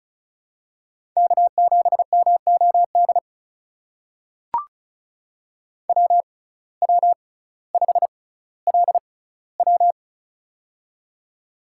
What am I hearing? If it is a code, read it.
Morse code, heard as a keyed tone, about 36 wpm, spaced as for 18 wpm, 700 Hz. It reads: K7MOD WW5LW
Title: USA Call Signs - 6x Character Spacing - 35wpm